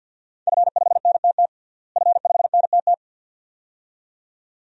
Morse code, heard as a keyed tone, about 52 wpm, 700 Hz. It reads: F5NTT F5NTT
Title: DXCC-100 Call Signs - 5 Characters - Repeated 2x - 50wpm